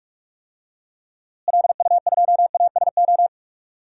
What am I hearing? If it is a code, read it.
Morse code, heard as a keyed tone, about 42 wpm, 700 Hz. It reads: LU2ASO
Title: DXCC-100 Call Signs - 6 Characters - Repeated 3x - 45wpm